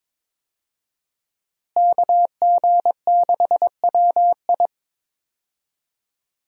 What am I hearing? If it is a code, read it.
Morse code, heard as a keyed tone, about 22 wpm, 700 Hz. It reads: KG6WI
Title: USA Call Signs - 5 Characters - Repeated 3x - 22wpm